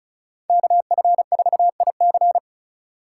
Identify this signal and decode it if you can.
Morse code, heard as a keyed tone, about 36 wpm, 700 Hz. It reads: KF4IC
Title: USA Call Signs - 5 Characters - 4x Character Spacing - 35wpm